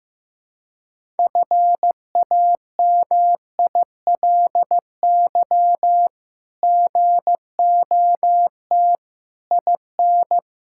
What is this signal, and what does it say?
Morse code, heard as a keyed tone, about 16 wpm, 700 Hz. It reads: FAMILY GOT IN
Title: Sentences from Top 1000 Words - Review Part 6 - 15wpm